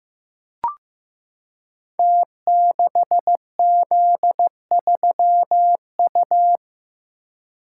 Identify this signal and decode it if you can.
Morse code, heard as a keyed tone, about 15 wpm, 700 Hz. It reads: T6Z3U
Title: Code Groups - 5 Letter:Numbers - 15wpm